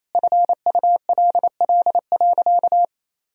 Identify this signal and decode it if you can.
Morse code, heard as a keyed tone, about 28 wpm, 700 Hz. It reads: FULL.
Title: Sentences from Top 1000 Words - Review Part 3 - 28wpm